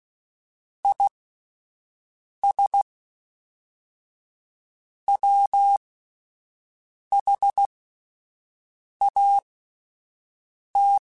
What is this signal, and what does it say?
Morse code, heard as a keyed tone, about 16 wpm, spaced as for 5 wpm, 800 Hz. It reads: IS WHAT